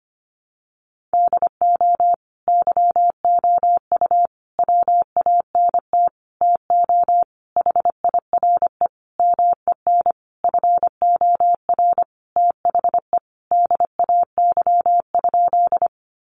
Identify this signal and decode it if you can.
Morse code, heard as a keyed tone, about 25 wpm, 700 Hz. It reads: DO YOU WANT TO HIRE MEN FOR THE DAY?
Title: Sentences from Top 1400 Words 25wpm